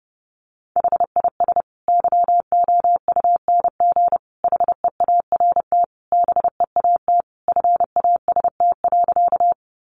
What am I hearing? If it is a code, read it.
Morse code, heard as a keyed tone, about 28 wpm, 700 Hz. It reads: HIS YOUNG HEART BEAT FAST.
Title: Sentences from Top 1300 Words 30wpm